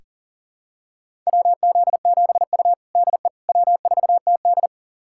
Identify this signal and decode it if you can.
Morse code, heard as a keyed tone, about 37 wpm, 700 Hz. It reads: WZ7U DE W4TD
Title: USA Callsign DE Callsign - 4 Characters - 40wpm